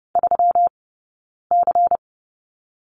Morse code, heard as a keyed tone, about 30 wpm, 700 Hz. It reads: 3 C
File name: Lesson 20d - Number 3 - Two Characters 30wpm